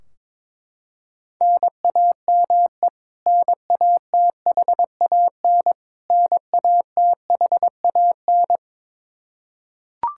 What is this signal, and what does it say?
Morse code, heard as a keyed tone, about 22 wpm, 700 Hz. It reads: NAME NATHAN NATHAN
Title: Practice QSO - Exchange 1 - Beginning - 22wpm